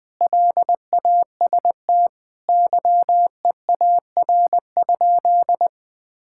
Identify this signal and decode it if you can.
Morse code, heard as a keyed tone, about 20 wpm, 700 Hz. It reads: LAST YEAR?
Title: Sentences from Top 1000 Words - Review Part 5 - 20wpm